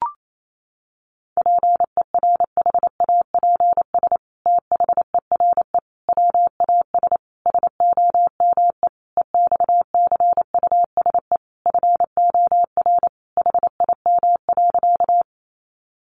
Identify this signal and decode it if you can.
Morse code, heard as a keyed tone, about 28 wpm, 700 Hz. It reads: PERHAPS THERE WAS SOME EXCUSE FOR HIM.
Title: Sentences from Top 700 Words 28wpm